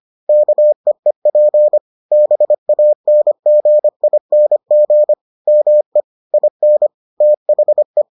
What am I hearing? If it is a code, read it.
Morse code, heard as a keyed tone, about 25 wpm, 600 Hz. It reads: KEEP BANGING ME IN THE